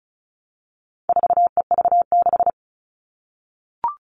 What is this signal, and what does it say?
Morse code, heard as a keyed tone, about 34 wpm, 700 Hz. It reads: 4EV6
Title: All Cut Numbers - 4-Digit 35wpm